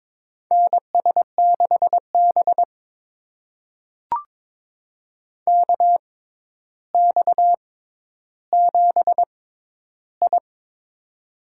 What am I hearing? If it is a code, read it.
Morse code, heard as a keyed tone, about 22 wpm, spaced as for 11 wpm, 700 Hz. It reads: NS6B KX7I
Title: USA Call Signs - 4 Characters - 6x Character Spacing - 22wpm